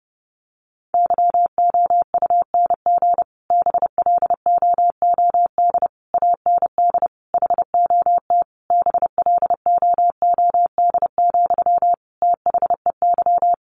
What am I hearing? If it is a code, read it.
Morse code, heard as a keyed tone, about 30 wpm, 700 Hz. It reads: YOUNG BLOOD AND HOT BLOOD, THEY
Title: Sentences from Top 1100 Words 30wpm